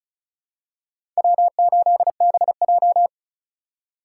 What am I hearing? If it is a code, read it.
Morse code, heard as a keyed tone, about 34 wpm, 700 Hz. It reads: W8BJ
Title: USA Call Signs - 4 Characters - 3x Character Spacing - 35wpm